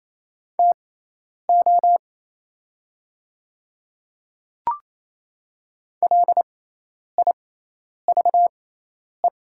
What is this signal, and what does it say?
Morse code, heard as a keyed tone, about 28 wpm, spaced as for 10 wpm, 700 Hz. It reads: TO LIVE
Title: 6x Character Spacing - Top 500 Words - 28wpm